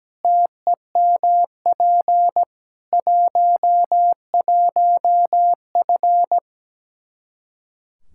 Morse code, heard as a keyed tone, about 17 wpm, 700 Hz. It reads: TEMP 11F
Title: Practice QSO - Exchange 2 - Info - 17wpm